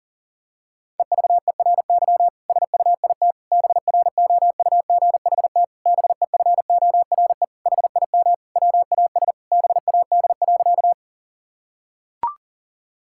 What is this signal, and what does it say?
Morse code, heard as a keyed tone, about 37 wpm, 700 Hz. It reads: EVERY SUIT BROUGHT BEFORE HIM WAS BAD.
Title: Sentences from Top 1600 Words 40wpm